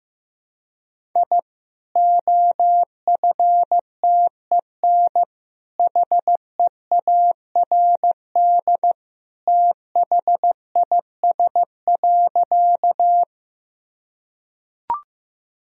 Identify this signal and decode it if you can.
Morse code, heard as a keyed tone, about 16 wpm, 700 Hz. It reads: I OFTEN HEARD THIS.
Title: Sentences from Top 1000 Words - Review Part 1 - 15wpm